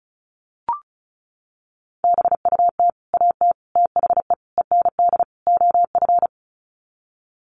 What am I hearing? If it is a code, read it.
Morse code, heard as a keyed tone, about 34 wpm, 700 Hz. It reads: BUT AT THE END OF
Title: Sets of 5 Words Encore2 35wpm